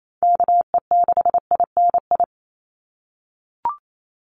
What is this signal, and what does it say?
Morse code, heard as a keyed tone, about 28 wpm, 700 Hz. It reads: KE6INI
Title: USA Call Signs 28wpm